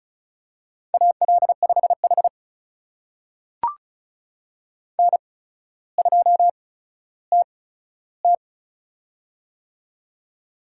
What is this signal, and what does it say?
Morse code, heard as a keyed tone, about 36 wpm, spaced as for 15 wpm, 700 Hz. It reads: AL5H N2TT
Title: USA Call Signs - 4 Characters - 8x Character Spacing - 35wpm